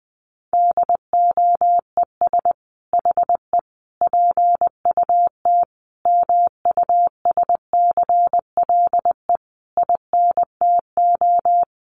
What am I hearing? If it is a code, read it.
Morse code, heard as a keyed tone, about 20 wpm, 700 Hz. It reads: DOES HE PUT MUSCLE INTO
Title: Sentences from Top 1300 Words 20wpm